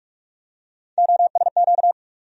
Morse code, heard as a keyed tone, about 43 wpm, 700 Hz. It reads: OSQ
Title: USA Call Sign Postfixes - 3 Letters 45wpm